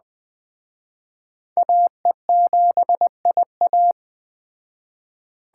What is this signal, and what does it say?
Morse code, heard as a keyed tone, about 18 wpm, 700 Hz. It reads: AE7IA
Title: Speed Racing - USA 2x2 Call Signs 15wpm